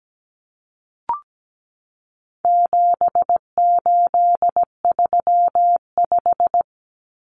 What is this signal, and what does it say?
Morse code, heard as a keyed tone, about 17 wpm, 700 Hz. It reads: 7835
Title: Numbers - 4-Digit 17wpm